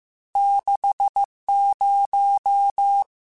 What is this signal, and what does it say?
Morse code, heard as a keyed tone, about 15 wpm, 800 Hz. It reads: 60
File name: Technical Subjects / Technical Subjects-mp3s / Sw Pwr Sup @13